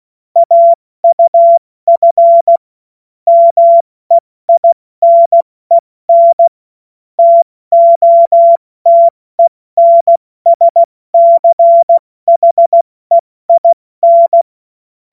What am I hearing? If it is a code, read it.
Morse code, heard as a keyed tone, about 16 wpm, 680 Hz. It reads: AUF MEINEN TOTENSCHEIN
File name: Morsekurs / Witze / Witz 22. Juni 2021, 70 F80